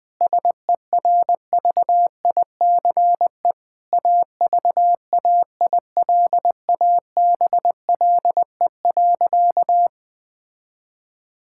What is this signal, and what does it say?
Morse code, heard as a keyed tone, about 20 wpm, 700 Hz. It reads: SERVICE AVAILABLE.
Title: Sentences from Top 1000 Words - Review Part 6 - 20wpm